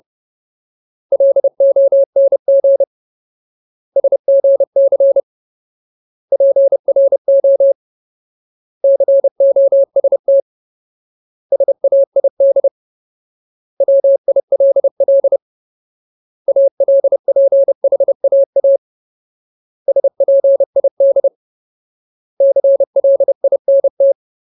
Morse code, heard as a keyed tone, about 30 wpm, 550 Hz. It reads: LONG SGC PRO COST SAID WILL ALPHAA SPID CLINT
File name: Ultimate Word List / 30WPM Ultimate